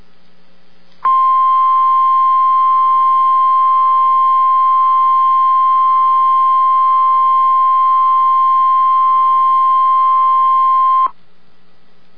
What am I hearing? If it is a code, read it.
Emergency Alert System (SAME) tones (the 1050 Hz attention tone). It is heard from 1.0 to 11.1 s.